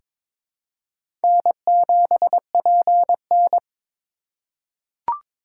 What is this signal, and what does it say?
Morse code, heard as a keyed tone, about 22 wpm, 700 Hz. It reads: N7PN
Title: USA Call Signs - 4 Characters - 2x Character Spacing - 22wpm